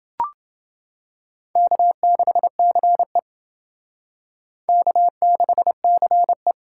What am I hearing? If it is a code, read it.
Morse code, heard as a keyed tone, about 28 wpm, 700 Hz. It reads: K6CE K6CE